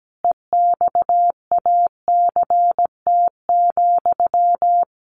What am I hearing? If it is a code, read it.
Morse code, heard as a keyed tone, about 17 wpm, 700 Hz. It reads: EXACT,
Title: Sentences from Top 1300 Words 17wpm